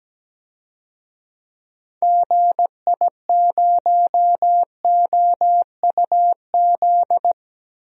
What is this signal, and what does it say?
Morse code, heard as a keyed tone, about 17 wpm, 700 Hz. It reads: GI0OUZ
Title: DXCC-100 Call Signs - 6 Characters - Repeated 3x - 17wpm